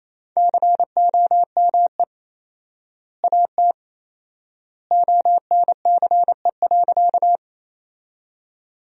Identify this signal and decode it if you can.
Morse code, heard as a keyed tone, about 28 wpm, 700 Hz. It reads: COME AT ONCE.
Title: 4x Word-Spacing - Sentences from Top 300 Words - 28wpm